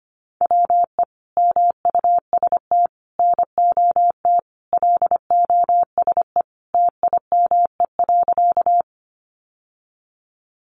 Morse code, heard as a keyed tone, about 25 wpm, 700 Hz. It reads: WE MUST NOT LOSE TIME.
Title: Sentences from Top 1100 Words 25wpm